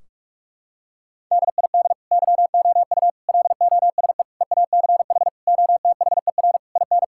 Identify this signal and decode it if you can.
Morse code, heard as a keyed tone, about 46 wpm, 700 Hz. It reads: DID YOU LOSE EACH OTHER IN
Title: Sentences from Top 1600 Words 45wpm